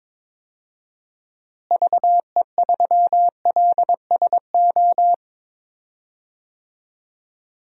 Morse code, heard as a keyed tone, about 22 wpm, 700 Hz. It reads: VE3LSO
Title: DXCC-100 Call Signs - 6 Characters - Repeated 2x - 22wpm